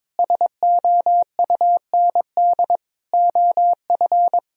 Morse code, heard as a keyed tone, about 22 wpm, 700 Hz. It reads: SOUND OF
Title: Sentences from Top 1000 Words - Review Part 1 - 22wpm